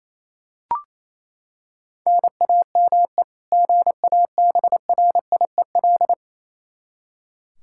Morse code, heard as a keyed tone, about 28 wpm, 700 Hz. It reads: NAME GABRIEL
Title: K1USN SST QSO Practice 28wpm